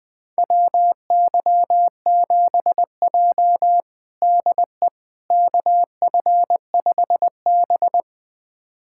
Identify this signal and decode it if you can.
Morse code, heard as a keyed tone, about 20 wpm, 700 Hz. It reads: WY7J DE KF5B
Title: USA Callsign DE Callsign - 4 Characters - 20wpm